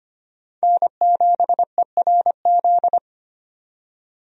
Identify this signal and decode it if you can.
Morse code, heard as a keyed tone, about 25 wpm, 700 Hz. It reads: N7ERZ
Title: USA Call Signs - 5 Characters - 25wpm